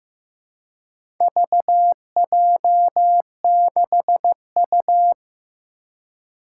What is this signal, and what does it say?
Morse code, heard as a keyed tone, about 15 wpm, 700 Hz. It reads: VJ6U
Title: Four Challenging Letter-Number Combinations 15wpm